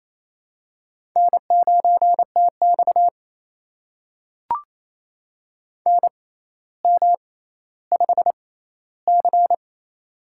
Morse code, heard as a keyed tone, about 28 wpm, spaced as for 14 wpm, 700 Hz. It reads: N9TX NM5C
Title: USA Call Signs - 4 Characters - 6x Character Spacing - 28wpm